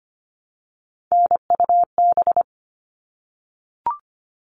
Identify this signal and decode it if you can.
Morse code, heard as a keyed tone, about 25 wpm, 700 Hz. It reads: NUB
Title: All Cut Numbers - 3-Digit 25wpm